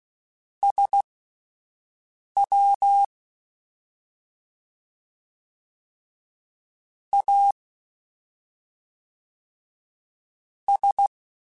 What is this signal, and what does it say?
Morse code, heard as a keyed tone, about 16 wpm, spaced as for 5 wpm, 800 Hz. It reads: SW A S
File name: Misc / Misc-mp3s / Jumps @5